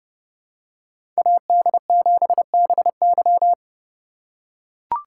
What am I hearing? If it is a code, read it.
Morse code, heard as a keyed tone, about 29 wpm, 700 Hz. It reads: AD7BY
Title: USA Call Signs - 5 Characters - 2x Character Spacing - 30wpm